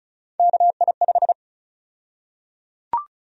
Morse code, heard as a keyed tone, about 34 wpm, 700 Hz. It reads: KI5
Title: USA Call Sign Prefixes - Repeated 2x - 35wpm